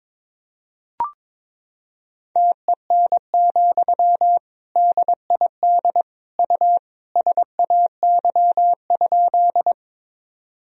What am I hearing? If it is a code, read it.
Morse code, heard as a keyed tone, about 22 wpm, 700 Hz. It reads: TEN, DID U SAY?
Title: Sentences from Top 400 Words and Abbreviations 22wpm